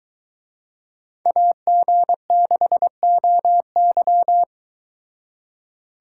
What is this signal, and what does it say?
Morse code, heard as a keyed tone, about 24 wpm, 700 Hz. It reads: AG6OY